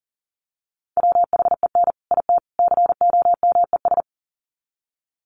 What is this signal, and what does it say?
Morse code, heard as a keyed tone, about 40 wpm, 700 Hz. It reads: WHEN IT COMES
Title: Sets of 3 Words 40wpm